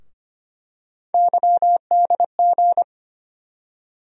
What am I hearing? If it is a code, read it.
Morse code, heard as a keyed tone, about 25 wpm, 700 Hz. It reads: YDG
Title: USA Call Sign Postfixes - 3 Letters 25wpm